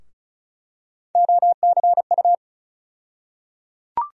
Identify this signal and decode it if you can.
Morse code, heard as a keyed tone, about 36 wpm, 700 Hz. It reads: OCU
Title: USA Call Sign Postfixes - 3 Letters 35wpm